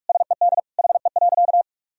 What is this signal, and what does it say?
Morse code, heard as a keyed tone, about 46 wpm, 700 Hz. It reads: SED HE.
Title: Sentences from Top 100 Words and Abbreviations 45wpm